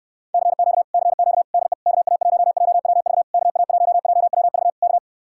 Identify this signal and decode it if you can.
Morse code, heard as a keyed tone, about 69 wpm, 700 Hz. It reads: CQ CQ DE KA1JWV KA1JWV K